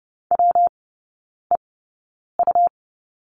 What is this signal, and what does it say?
Morse code, heard as a keyed tone, about 30 wpm, spaced as for 9 wpm, 700 Hz. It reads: WEU
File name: Lesson 19e - Letter w - Three Characters 30wpm